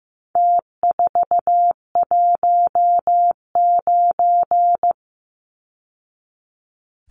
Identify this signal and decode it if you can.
Morse code, heard as a keyed tone, about 15 wpm, 700 Hz. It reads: T419
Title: Numbers - Contest Style - 4-Digit 15wpm